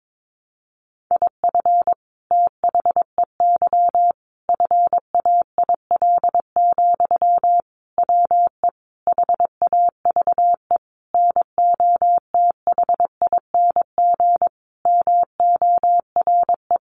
Morse code, heard as a keyed tone, about 22 wpm, 700 Hz. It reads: IF THEY FAIL, WE HAVE NOTHING MORE
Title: Sentences from Top 1100 Words 22wpm